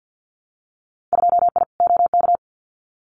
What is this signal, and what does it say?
Morse code, heard as a keyed tone, about 50 wpm, 700 Hz. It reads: 2I OK
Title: Winter Field Day Exchanges Encore 50wpm